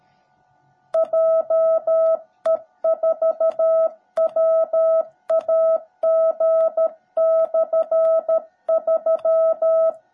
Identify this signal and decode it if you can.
Morse code, heard as a keyed tone, about 13 wpm, 660 Hz. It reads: JE4WAG/3
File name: jh8cbh / 2021cw / cw17b